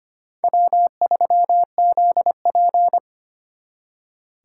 Morse code, heard as a keyed tone, about 32 wpm, 700 Hz. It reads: W3ZP